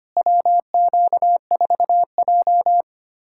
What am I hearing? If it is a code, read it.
Morse code, heard as a keyed tone, about 25 wpm, 700 Hz. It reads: WQ4J
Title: USA Callsign DE Callsign - 4 Characters - 25wpm